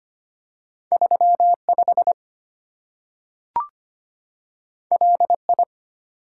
Morse code, heard as a keyed tone, about 25 wpm, 700 Hz. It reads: 35 LI
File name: Two Letter-Number Combinations - 25wpm